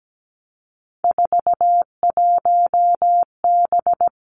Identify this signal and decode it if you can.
Morse code, heard as a keyed tone, about 17 wpm, 700 Hz. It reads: 41B
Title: Three Challenging Letter-Number Combinations 17wpm